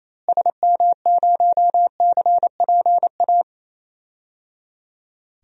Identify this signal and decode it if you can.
Morse code, heard as a keyed tone, about 28 wpm, 700 Hz. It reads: SM0CPA
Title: DXCC-100 Call Signs - 6 Characters - Repeated 3x - 28wpm